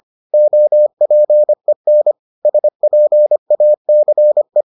Morse code, heard as a keyed tone, about 25 wpm, 600 Hz. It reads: OPEN SPACE